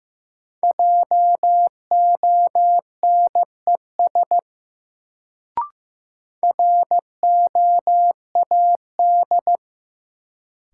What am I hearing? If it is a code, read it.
Morse code, heard as a keyed tone, about 15 wpm, 700 Hz. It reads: JONES ROAD
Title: Sets of 5 Words - New Word Drill - 15wpm